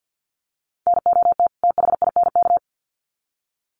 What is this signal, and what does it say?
Morse code, heard as a keyed tone, about 50 wpm, 700 Hz. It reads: NOT THINK
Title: Sets of 2 Words 50wpm